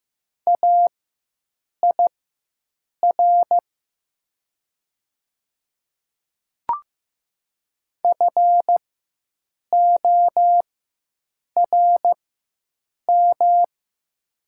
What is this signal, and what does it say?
Morse code, heard as a keyed tone, about 15 wpm, spaced as for 7 wpm, 700 Hz. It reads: AIR FORM
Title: 4x Character Spacing - Top 500 Words - 15wpm